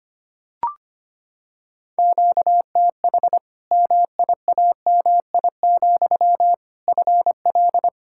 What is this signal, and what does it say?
Morse code, heard as a keyed tone, about 25 wpm, 700 Hz. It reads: QTH MIAMI, FL
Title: Practice QSO - Exchange 1 - Beginning - 25wpm